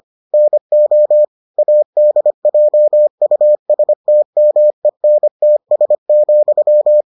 Morse code, heard as a keyed tone, about 25 wpm, 600 Hz. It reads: NO ADJUSTMENTS,